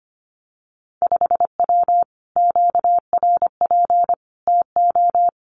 Morse code, heard as a keyed tone, about 25 wpm, 700 Hz. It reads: 5W QRP TO
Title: Common QSO Phrases Encore 25wpm